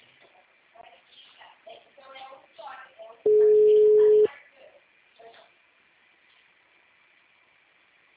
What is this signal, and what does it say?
Ringback tone (Europe, 425 Hz), with speech.